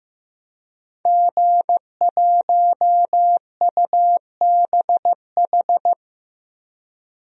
Morse code, heard as a keyed tone, about 16 wpm, 700 Hz. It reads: G1UBH
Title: DXCC-100 Call Signs - 5 Characters - Repeated 3x - 15wpm